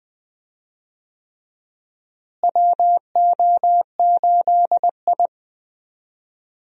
Morse code, heard as a keyed tone, about 20 wpm, 700 Hz. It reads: WO8I
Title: USA Call Signs - 4 Characters - Repeated 3x - 20wpm